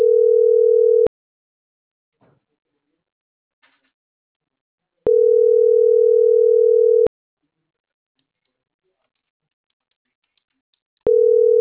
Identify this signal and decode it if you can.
Ringback tone (North America), 440 + 480 Hz.